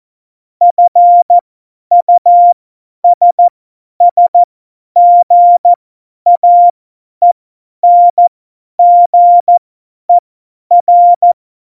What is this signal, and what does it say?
Morse code, heard as a keyed tone, about 14 wpm, spaced as for 10 wpm, 700 Hz. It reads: FUSSGAENGER
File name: Morsekurs / Witze / Witz 08. Januar 2019, 50 F70